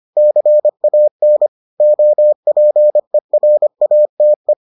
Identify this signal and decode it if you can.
Morse code, heard as a keyed tone, about 25 wpm, 600 Hz. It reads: CAN OPERATE